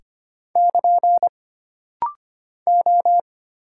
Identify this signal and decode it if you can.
Morse code, heard as a keyed tone, about 25 wpm, 700 Hz. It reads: <KN> O
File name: Single Character (Letter:Number:Punctuation:Prosign) - Rapid-Fire - 25wpm